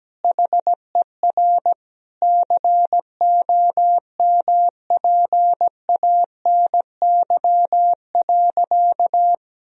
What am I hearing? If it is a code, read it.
Morse code, heard as a keyed tone, about 17 wpm, 700 Hz. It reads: HER COMPANY.
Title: Sentences from Top 1000 Words - Review Part 1 - 17wpm